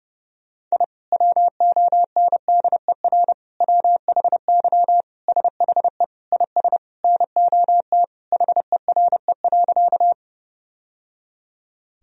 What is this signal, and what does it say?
Morse code, heard as a keyed tone, about 30 wpm, 700 Hz. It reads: I WONDER WHY SHE IS NOT HERE.
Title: Sentences from Top 1000 Words - Review Part 3 - 30wpm